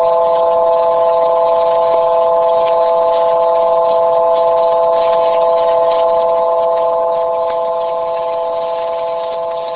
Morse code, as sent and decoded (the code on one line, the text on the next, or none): none